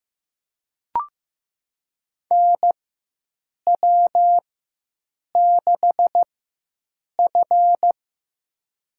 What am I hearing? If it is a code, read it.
Morse code, heard as a keyed tone, about 15 wpm, spaced as for 7 wpm, 700 Hz. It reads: NW6F
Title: USA Call Signs - 4x Character Spacing - 15wpm